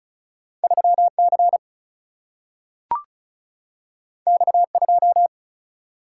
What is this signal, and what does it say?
Morse code, heard as a keyed tone, about 36 wpm, 700 Hz. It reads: 3C X2
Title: Single-Letter Single-Number 35wpm